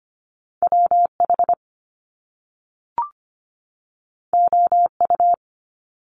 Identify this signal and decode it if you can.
Morse code, heard as a keyed tone, about 25 wpm, 700 Hz. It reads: WH OU
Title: Top Two Letter Combinations in Words 25wpm